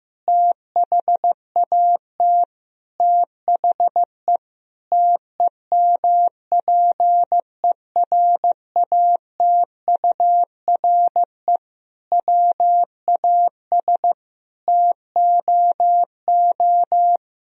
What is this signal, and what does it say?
Morse code, heard as a keyed tone, about 15 wpm, 700 Hz. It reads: THAT THE TEMPERATURE WAS TOO
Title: Sentences from Top 1000 Words - Review Part 4 - 15wpm